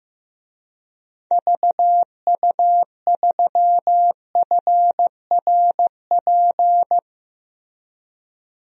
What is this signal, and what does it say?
Morse code, heard as a keyed tone, about 15 wpm, 700 Hz. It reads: VU3FRP
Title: DXCC-100 Call Signs - 6 Characters - Repeated 3x - 15wpm